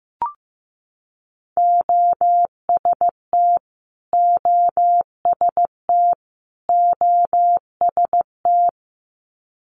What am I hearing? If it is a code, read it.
Morse code, heard as a keyed tone, about 15 wpm, 700 Hz. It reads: OST OST OST
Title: Multi-letter Phonemes Repeated 3x 15wpm